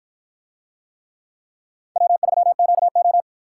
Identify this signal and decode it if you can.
Morse code, heard as a keyed tone, about 53 wpm, 700 Hz. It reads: W3QO